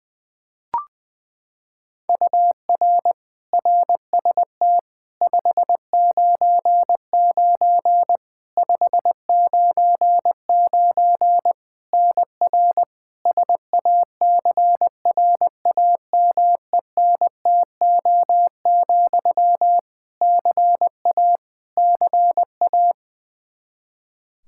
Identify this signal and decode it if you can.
Morse code, heard as a keyed tone, about 20 wpm, 700 Hz. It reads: UR RST 599 599 NR SACRAMENTO, CA CA
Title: K1USN SST QSO Practice 20wpm